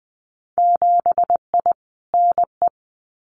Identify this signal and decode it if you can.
Morse code, heard as a keyed tone, about 20 wpm, 700 Hz. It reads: 7I NE
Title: Winter Field Day Exchanges Encore 20wpm